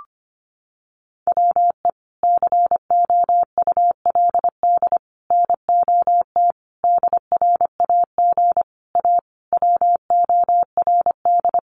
Morse code, heard as a keyed tone, about 25 wpm, 700 Hz. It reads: WE COULD NOT DRAG A WORD
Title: Sentences from Top 1500 Words 25wpm